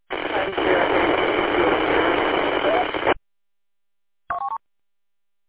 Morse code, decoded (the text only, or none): none